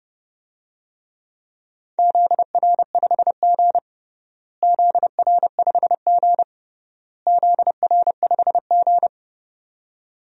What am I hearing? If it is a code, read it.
Morse code, heard as a keyed tone, about 30 wpm, 700 Hz. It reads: ZR5G ZR5G ZR5G
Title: DXCC-100 Call Signs - 4 Characters - Repeated 3x - 30wpm